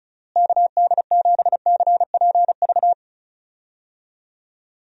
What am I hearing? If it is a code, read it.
Morse code, heard as a keyed tone, about 36 wpm, 700 Hz. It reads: KD7CPV
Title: USA Call Signs - 6 Characters - Repeated 2x - 35wpm